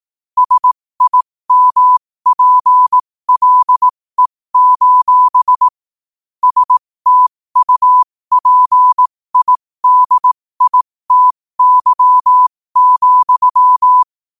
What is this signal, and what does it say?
Morse code, heard as a keyed tone, about 18 wpm, spaced as for 15 wpm, 1000 Hz. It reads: SIMPLE: STUPIDITY,
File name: qotdfiles / 15wpm